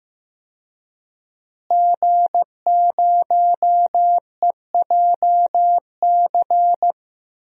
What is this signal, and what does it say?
Morse code, heard as a keyed tone, about 15 wpm, 700 Hz. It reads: G0EJC
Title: DXCC-100 Call Signs - 5 Characters - Repeated 3x - 15wpm